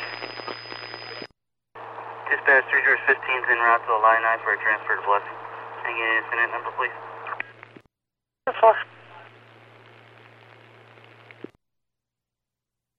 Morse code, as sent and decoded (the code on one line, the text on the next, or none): none